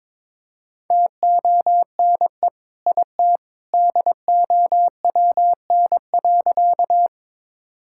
Morse code, heard as a keyed tone, about 22 wpm, 700 Hz. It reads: TONE IT DOWN.
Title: Sentences from Top 1600 Words 22wpm